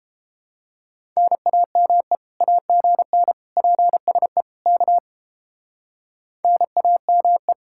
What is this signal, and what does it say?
Morse code, heard as a keyed tone, about 30 wpm, 700 Hz. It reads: NAME AGN PSE K NAME